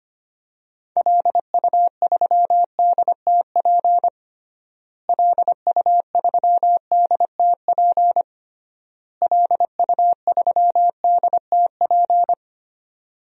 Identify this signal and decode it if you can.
Morse code, heard as a keyed tone, about 25 wpm, 700 Hz. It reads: LU3DTP LU3DTP LU3DTP
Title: DXCC-100 Call Signs - 6 Characters - Repeated 3x - 25wpm